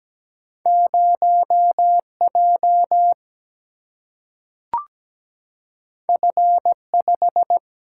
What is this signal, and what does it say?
Morse code, heard as a keyed tone, about 17 wpm, 700 Hz. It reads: 0J F5
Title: Single-Letter Single-Number 17wpm